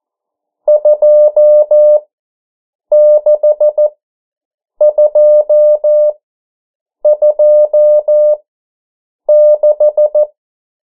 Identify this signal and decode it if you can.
Morse code, heard as a keyed tone, about 14 wpm, spaced as for 7 wpm, 600 Hz. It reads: 26226
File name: morse mp3 / 09 urok 8